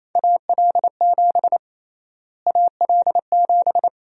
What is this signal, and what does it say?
Morse code, heard as a keyed tone, about 28 wpm, 700 Hz. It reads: AL7 AL7
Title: USA Call Sign Prefixes - Repeated 2x - 28wpm